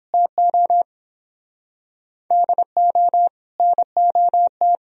Morse code, heard as a keyed tone, about 31 wpm, 700 Hz. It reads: TO DO NOT